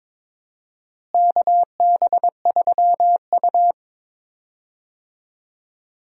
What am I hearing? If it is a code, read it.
Morse code, heard as a keyed tone, about 22 wpm, 700 Hz. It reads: KB3U
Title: USA Call Signs - 4 Characters - Repeated 2x - 22wpm